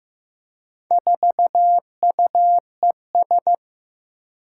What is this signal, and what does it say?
Morse code, heard as a keyed tone, about 15 wpm, 700 Hz. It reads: 4UES
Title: Four Challenging Letter-Number Combinations 15wpm